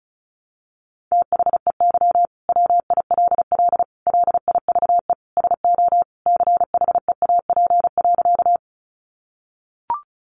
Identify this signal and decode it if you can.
Morse code, heard as a keyed tone, about 34 wpm, 700 Hz. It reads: THEY WILL LIVE SO CHEAP.
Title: Sentences from Top 1200 Words 35wpm